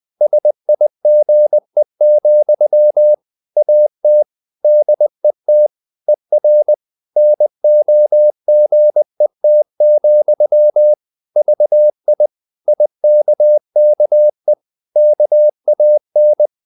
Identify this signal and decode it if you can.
Morse code, heard as a keyed tone, about 20 wpm, 600 Hz. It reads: SIGE, AT DET ER NOGET, VI IKKE KAN
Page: Ferie 20 wpm